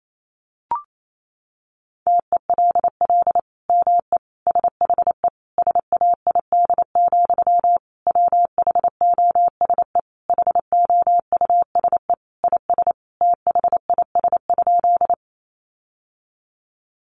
Morse code, heard as a keyed tone, about 28 wpm, 700 Hz. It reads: TELL ME SHE SAID, WHOSE HOUSE IS THIS?
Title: Sentences from Top 1200 Words 28wpm